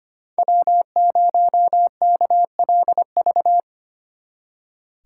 Morse code, heard as a keyed tone, about 30 wpm, 700 Hz. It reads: W0KLV
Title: Speed Racing - USA 1x3 Call Signs 25wpm